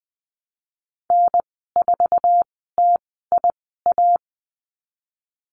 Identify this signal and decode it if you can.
Morse code, heard as a keyed tone, about 20 wpm, spaced as for 15 wpm, 700 Hz. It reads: N4TIA
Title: 048 Lesson 9g - Number 4 - Call Signs 20wpm